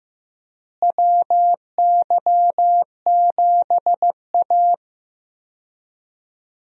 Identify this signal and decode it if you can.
Morse code, heard as a keyed tone, about 15 wpm, 700 Hz. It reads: WY7A
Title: USA Call Signs - 4 Characters - Repeated 3x - 15wpm